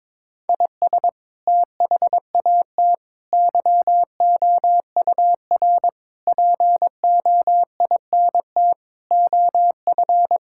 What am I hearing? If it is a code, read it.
Morse code, heard as a keyed tone, about 22 wpm, 700 Hz. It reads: IS THAT YOUR POINT OF
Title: Sentences from Top 1000 Words - Review Part 6 - 22wpm